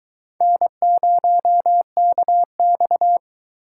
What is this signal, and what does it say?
Morse code, heard as a keyed tone, about 22 wpm, 700 Hz. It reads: N0KX